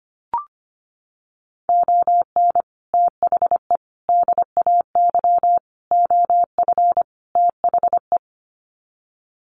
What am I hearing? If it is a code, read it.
Morse code, heard as a keyed tone, about 25 wpm, 700 Hz. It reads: ON THE DAY OF THE
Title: Sets of 5 Words Encore 25wpm